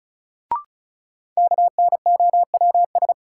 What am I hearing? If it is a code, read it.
Morse code, heard as a keyed tone, about 34 wpm, 700 Hz. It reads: KNOWS
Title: Sentences from Top 1000 Words - Review Part 1 - New Word Drill - 35wpm